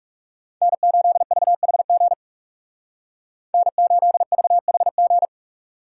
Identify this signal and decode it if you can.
Morse code, heard as a keyed tone, about 37 wpm, 700 Hz. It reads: N8VHG N8VHG
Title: Speed Racing - USA 1x3 Call Signs 30wpm